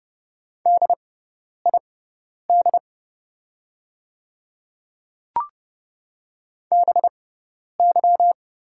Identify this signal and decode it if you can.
Morse code, heard as a keyed tone, about 30 wpm, spaced as for 10 wpm, 700 Hz. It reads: DID BY
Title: 6x Character Spacing - Top 200 Words - 30wpm